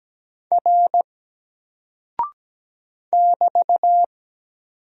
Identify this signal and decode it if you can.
Morse code, heard as a keyed tone, about 17 wpm, 700 Hz. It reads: R <BT>
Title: Single Character (Letter:Number:Punctuation:Prosign) Alt 17wpm